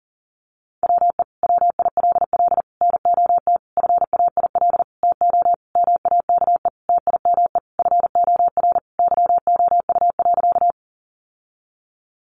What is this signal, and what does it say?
Morse code, heard as a keyed tone, about 37 wpm, 700 Hz. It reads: WE WILL NOT FAIL TO MAKE TIME FOR YOU.
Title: Sentences from Top 1100 Words 40wpm